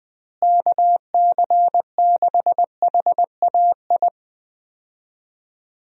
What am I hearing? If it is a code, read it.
Morse code, heard as a keyed tone, about 21 wpm, 700 Hz. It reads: KC6HAI
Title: USA Call Signs - 6 Characters - Repeated 3x - 20wpm